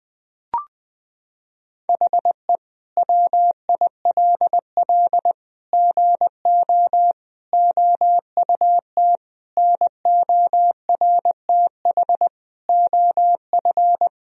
Morse code, heard as a keyed tone, about 21 wpm, 700 Hz. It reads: HE WILL GO OUT NORTH OF
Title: Sentences from Top 1000 Words - Review Part 1 - 20wpm